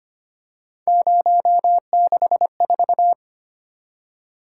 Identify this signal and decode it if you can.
Morse code, heard as a keyed tone, about 25 wpm, 700 Hz. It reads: 064
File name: Three Letter-Number Combinations - 25wpm